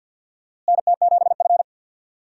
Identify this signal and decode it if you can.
Morse code, heard as a keyed tone, about 51 wpm, 700 Hz. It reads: NT7F
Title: USA Call Signs - 4 Characters - Repeated 3x - 50wpm